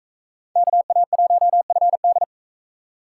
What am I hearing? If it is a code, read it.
Morse code, heard as a keyed tone, about 44 wpm, 700 Hz. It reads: KA1FD